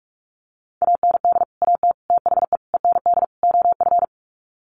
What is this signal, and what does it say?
Morse code, heard as a keyed tone, about 44 wpm, 700 Hz. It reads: AND AT THE END OF
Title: Sets of 5 Words 45wpm